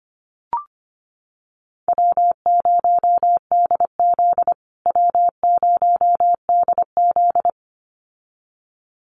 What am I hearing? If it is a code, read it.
Morse code, heard as a keyed tone, about 25 wpm, 700 Hz. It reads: W0DZ W0DZ
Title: USA Call Signs Repeated 2x 25wpm